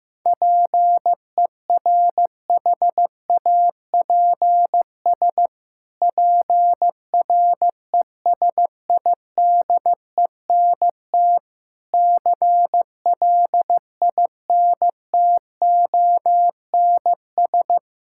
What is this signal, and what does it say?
Morse code, heard as a keyed tone, about 16 wpm, 700 Hz. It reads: PERHAPS PRESIDENT CLINTONS
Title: Pangram Challenge v2 - 15wpm